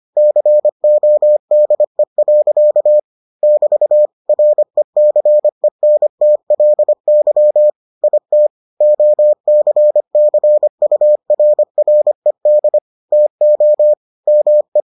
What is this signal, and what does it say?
Morse code, heard as a keyed tone, about 25 wpm, 600 Hz. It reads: CODE. <BT> RECENTLY IT OCCURRED TO ME